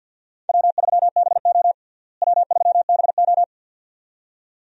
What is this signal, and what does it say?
Morse code, heard as a keyed tone, about 47 wpm, 700 Hz. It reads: W3BO W3BO
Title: USA Call Signs - 4 Characters - Repeated 2x - 50wpm